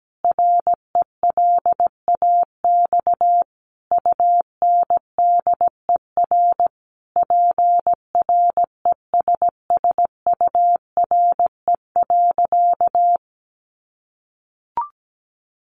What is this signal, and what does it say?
Morse code, heard as a keyed tone, about 17 wpm, 700 Hz. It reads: RELAX UNDER PRESSURE.
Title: Sentences from Top 1400 Words 17wpm